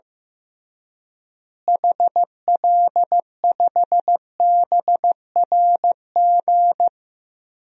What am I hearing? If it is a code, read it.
Morse code, heard as a keyed tone, about 15 wpm, 700 Hz. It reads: HL5BRG
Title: DXCC-100 Call Signs - 6 Characters - Repeated 3x - 15wpm